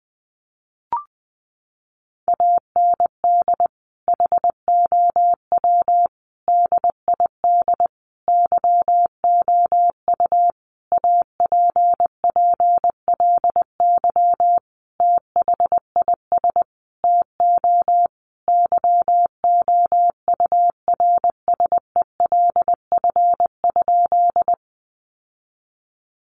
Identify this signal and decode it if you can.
Morse code, heard as a keyed tone, about 20 wpm, 700 Hz. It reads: AND HOW DID YOU APPLY THIS TO YOURSELF?
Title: Sentences from Top 1200 Words 20wpm